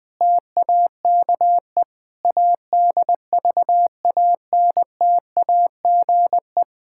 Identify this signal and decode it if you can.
Morse code, heard as a keyed tone, about 20 wpm, 700 Hz. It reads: TAKE ADVANTAGE
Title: Sentences with 5-Word Phrases - 20wpm